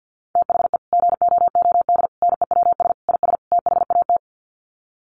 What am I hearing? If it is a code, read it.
Morse code, heard as a keyed tone, about 50 wpm, 700 Hz. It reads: THE GOOD NEWS IS THAT
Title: Sets of 5 Words 50wpm